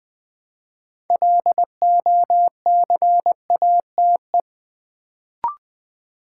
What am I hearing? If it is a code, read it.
Morse code, heard as a keyed tone, about 20 wpm, 700 Hz. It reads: LOCATE
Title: Build A Word - 2 Letters At A Time - 6-Letter Words - 20wpm